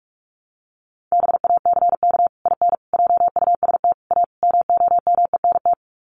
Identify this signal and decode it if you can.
Morse code, heard as a keyed tone, about 44 wpm, 700 Hz. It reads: BACK IN JUST A MOMENT
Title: Sets of 5 Words Encore 45wpm